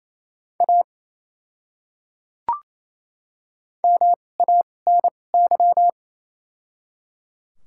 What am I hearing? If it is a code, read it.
Morse code, heard as a keyed tone, about 27 wpm, spaced as for 20 wpm, 700 Hz. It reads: A MANY
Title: 2x Character Spacing - Top 100 Words - 28wpm